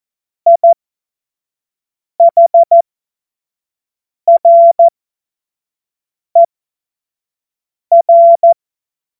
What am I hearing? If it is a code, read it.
Morse code, heard as a keyed tone, about 14 wpm, spaced as for 5 wpm, 680 Hz. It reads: IHRER
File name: Morsekurs / Witze / Witz 20. Dezember 2022 25 F70